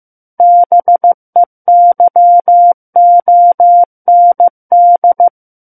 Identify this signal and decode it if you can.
Morse code, heard as a keyed tone, about 15 wpm, 700 Hz. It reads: BEYOND
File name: Book One / 06-Chapter 6 - 15 WPM